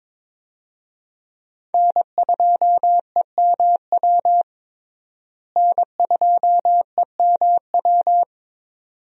Morse code, heard as a keyed tone, about 22 wpm, 700 Hz. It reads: N2EMW N2EMW
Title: USA Call Signs - 5 Characters - Repeated 2x - 22wpm